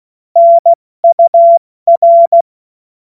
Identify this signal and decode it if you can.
Morse code, heard as a keyed tone, about 16 wpm, 680 Hz. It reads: NUR
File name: Morsekurs / Witze / Witz 18. Juli 2023 70 F80